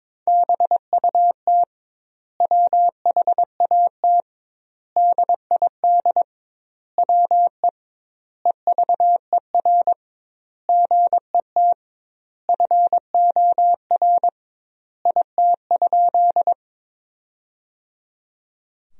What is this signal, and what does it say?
Morse code, heard as a keyed tone, about 23 wpm, 700 Hz. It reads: BUT WHAT DID WE EVER GET FOR IT?
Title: 2x Word-Spacing - Sentences from Top 400 Words - 22wpm